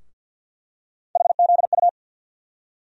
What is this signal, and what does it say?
Morse code, heard as a keyed tone, about 53 wpm, 700 Hz. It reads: HZU
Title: USA Call Sign Postfixes - 3 Letters 50wpm